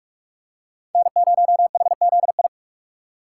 Morse code, heard as a keyed tone, about 45 wpm, 700 Hz. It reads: N0HZI